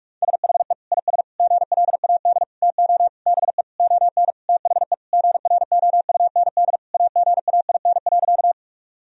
Morse code, heard as a keyed tone, about 46 wpm, 700 Hz. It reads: SHE IS GLAD TO BE ON THE GROUND AGAIN.
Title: Sentences from Top 1000 Words - Review Part 2 - 45wpm